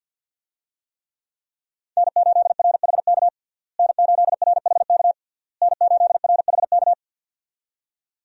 Morse code, heard as a keyed tone, about 53 wpm, 700 Hz. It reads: N8RHK N8RHK N8RHK
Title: USA Call Signs - 5 Characters - Repeated 3x - 50wpm